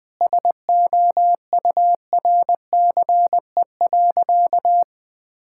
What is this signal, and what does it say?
Morse code, heard as a keyed tone, about 20 wpm, 700 Hz. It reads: SOURCE.
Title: Sentences from Top 1000 Words - Review Part 5 - 20wpm